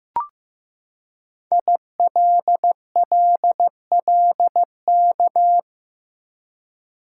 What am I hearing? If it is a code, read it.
Morse code, heard as a keyed tone, about 15 wpm, 700 Hz. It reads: ILLLK
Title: Code Groups - 5 Letters - 15wpm